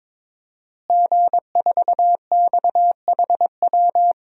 Morse code, heard as a keyed tone, about 22 wpm, 700 Hz. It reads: G4XHW
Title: DXCC-100 Call Signs - 5 Characters - Repeated 2x - 22wpm